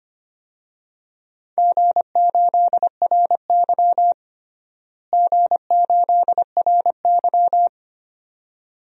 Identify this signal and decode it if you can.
Morse code, heard as a keyed tone, about 25 wpm, 700 Hz. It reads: G8RY G8RY
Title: DXCC-100 Call Signs - 4 Characters - Repeated 2x - 25wpm